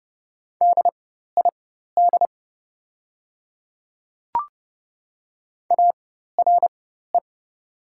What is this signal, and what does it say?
Morse code, heard as a keyed tone, about 30 wpm, spaced as for 14 wpm, 700 Hz. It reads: DID ARE
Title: 4x Character Spacing - Top 200 Words - 30wpm